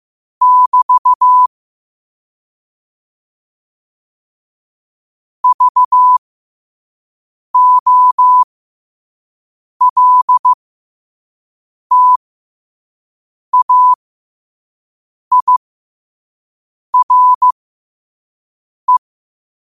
Morse code, heard as a keyed tone, about 15 wpm, spaced as for 5 wpm, 1000 Hz. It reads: <BT> VOLTAIRE